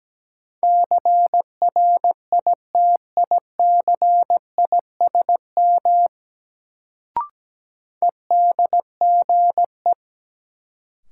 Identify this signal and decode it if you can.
Morse code, heard as a keyed tone, about 17 wpm, 700 Hz. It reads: CRITICISM EDGE
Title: Sentences from Top 1300 Words - New Word Drill Alt - 17wpm